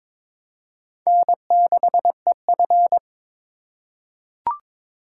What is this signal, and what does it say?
Morse code, heard as a keyed tone, about 22 wpm, 700 Hz. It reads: N6EF
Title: USA Call Signs - 4 Characters - 4x Character Spacing - 22wpm